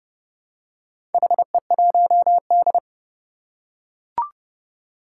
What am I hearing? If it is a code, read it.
Morse code, heard as a keyed tone, about 29 wpm, 700 Hz. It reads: HE1D
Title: Four Challenging Letter-Number Combinations 30wpm